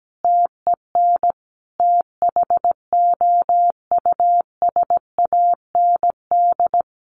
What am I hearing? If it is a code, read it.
Morse code, heard as a keyed tone, about 17 wpm, 700 Hz. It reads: TEN THOUSAND
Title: A Sound Of Thunder 17wpm